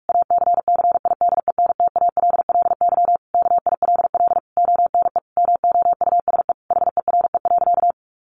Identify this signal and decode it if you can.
Morse code, heard as a keyed tone, about 44 wpm, 700 Hz. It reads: ACCIDENTALLY KILL ONE MOUSE HERE.
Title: A Sound Of Thunder 45wpm